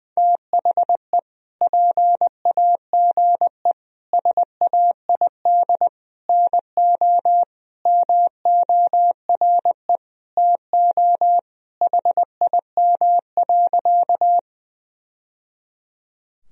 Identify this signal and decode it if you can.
Morse code, heard as a keyed tone, about 20 wpm, 700 Hz. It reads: THE PAGE SAID NO MORE TO HIM.
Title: Sentences from Top 1000 Words - Review Part 1 - 20wpm